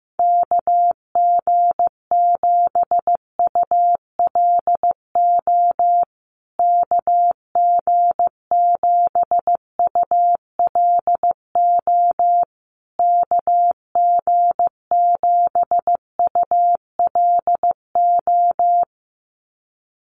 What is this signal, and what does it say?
Morse code, heard as a keyed tone, about 15 wpm, 700 Hz. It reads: KG7ULO KG7ULO KG7ULO
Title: USA Call Signs Repeated 3x 15wpm